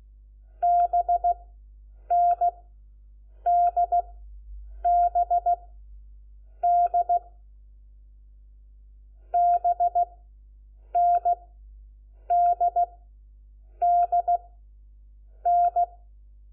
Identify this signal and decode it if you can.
Morse code, heard as a keyed tone, about 16 wpm, spaced as for 7 wpm, 700 Hz. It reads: BNDBD BNDDN
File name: Lecciones / CLASE Nº4